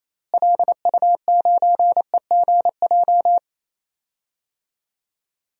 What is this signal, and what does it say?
Morse code, heard as a keyed tone, about 27 wpm, 700 Hz. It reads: LU9EGJ
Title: DXCC-100 Call Signs - 6 Characters - Repeated 2x - 28wpm